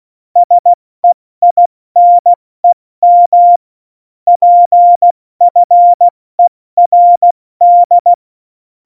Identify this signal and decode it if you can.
Morse code, heard as a keyed tone, about 16 wpm, 700 Hz. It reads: SEINEM PFERD
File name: Morsekurs / Witze / Witz 11 August 2020, 70 F80